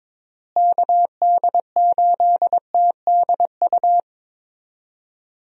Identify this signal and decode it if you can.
Morse code, heard as a keyed tone, about 22 wpm, 700 Hz. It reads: KD8TDU
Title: USA Call Signs - 6 Characters - Repeated 3x - 22wpm